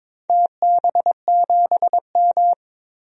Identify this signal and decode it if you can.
Morse code, heard as a keyed tone, about 22 wpm, 700 Hz. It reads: TB7M
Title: DXCC-100 Call Signs - 4 Characters - 22wpm